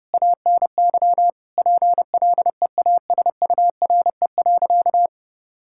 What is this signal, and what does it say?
Morse code, heard as a keyed tone, about 31 wpm, 700 Hz. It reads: ANY PLEASURE.
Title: Sentences from Top 1600 Words 30wpm